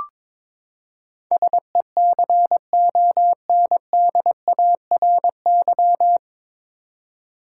Morse code, heard as a keyed tone, about 21 wpm, 700 Hz. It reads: SECONDARY
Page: Morse Code Ninja practice 22wpm